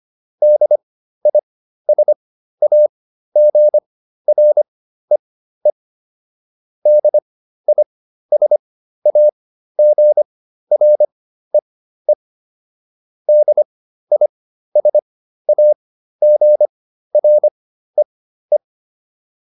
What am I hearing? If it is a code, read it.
Morse code, heard as a keyed tone, about 25 wpm, spaced as for 13 wpm, 600 Hz. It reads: DISAGREE DISAGREE DISAGREE